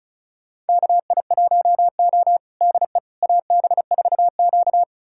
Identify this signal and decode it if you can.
Morse code, heard as a keyed tone, about 36 wpm, 700 Hz. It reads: KI1O DE AB4Q
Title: USA Callsign DE Callsign - 4 Characters - 35wpm